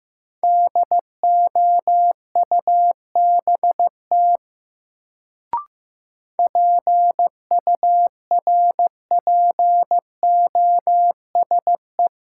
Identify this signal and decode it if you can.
Morse code, heard as a keyed tone, about 15 wpm, 700 Hz. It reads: DOUBT PURPOSE
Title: Sets of 5 Words - New Word Drill - 15wpm